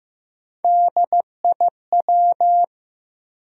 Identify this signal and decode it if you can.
Morse code, heard as a keyed tone, about 15 wpm, 700 Hz. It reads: DIW
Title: USA Call Sign Postfixes - 3 Letters 15wpm